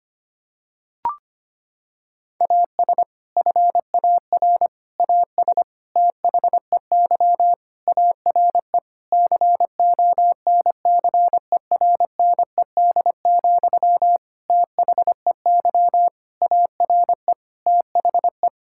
Morse code, heard as a keyed tone, about 25 wpm, 700 Hz. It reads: AS FAR AS THEY ARE CONCERNED, THEY ARE THE
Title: Sentences from Top 1000 Words - Review Part 6 - 25wpm